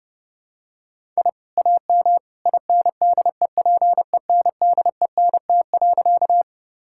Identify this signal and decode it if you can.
Morse code, heard as a keyed tone, about 32 wpm, 700 Hz. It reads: I AM INDEPENDENT.
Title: Sentences from Top 1000 Words - Review Part 6 - 30wpm